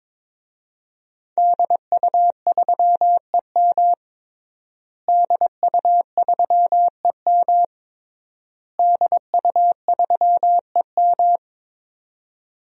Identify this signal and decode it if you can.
Morse code, heard as a keyed tone, about 22 wpm, 700 Hz. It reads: DU3EM DU3EM DU3EM
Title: DXCC-100 Call Signs - 5 Characters - Repeated 3x - 22wpm